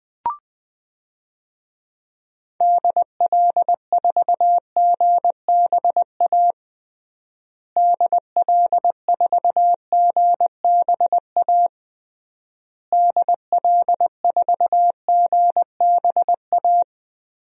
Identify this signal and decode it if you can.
Morse code, heard as a keyed tone, about 20 wpm, 700 Hz. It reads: DL4GBA DL4GBA DL4GBA
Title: DXCC-100 Call Signs - 6 Characters - Repeated 3x - 20wpm